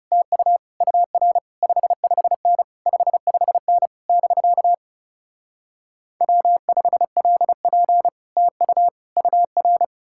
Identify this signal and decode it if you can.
Morse code, heard as a keyed tone, about 38 wpm, 700 Hz. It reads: TU UR 55N 55N <BK> W5LP TU UR
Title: Speed-Racing Common QSO Phrases - 30wpm